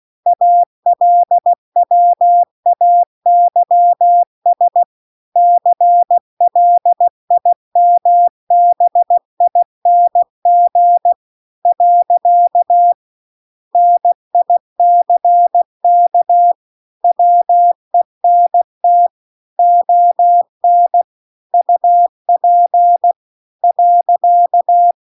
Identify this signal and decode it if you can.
Morse code, heard as a keyed tone, about 16 wpm, 700 Hz. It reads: ALWAYS CLIMBING . NICK WENT ON UP .